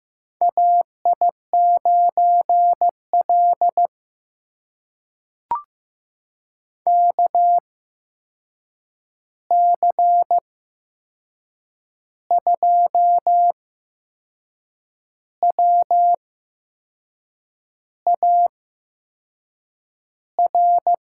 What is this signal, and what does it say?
Morse code, heard as a keyed tone, about 15 wpm, spaced as for 6 wpm, 700 Hz. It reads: AI9L KC2WAR
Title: USA Call Signs - 8x Character Spacing - 15wpm